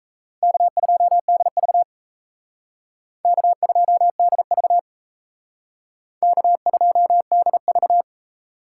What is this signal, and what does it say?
Morse code, heard as a keyed tone, about 36 wpm, 700 Hz. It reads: K2DV K2DV K2DV